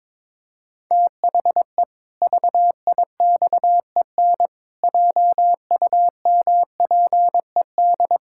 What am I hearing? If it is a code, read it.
Morse code, heard as a keyed tone, about 21 wpm, 700 Hz. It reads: THE VIXEN JUMPED
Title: Pangram Challenge v2 - 22wpm